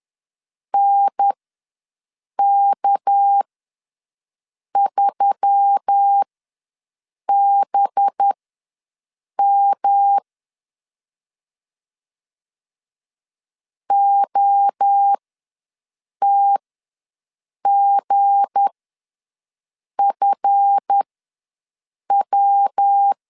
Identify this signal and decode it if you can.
Morse code, heard as a keyed tone, about 10 wpm, spaced as for 6 wpm, 800 Hz. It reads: NK3BM OTGFW
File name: LEZIONI / 7 GF9Z / 7 GF ws 10wpm 3918